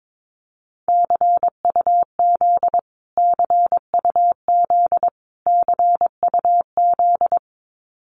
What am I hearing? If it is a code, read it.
Morse code, heard as a keyed tone, about 22 wpm, 700 Hz. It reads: CUZ CUZ CUZ
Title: Abbreviations Repeated 3x - 22wpm